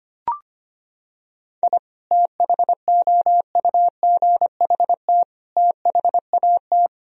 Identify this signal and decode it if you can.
Morse code, heard as a keyed tone, about 25 wpm, 700 Hz. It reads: I THOUGHT THAT
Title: Sentences from Top 1000 Words - Review Part 4 - 25wpm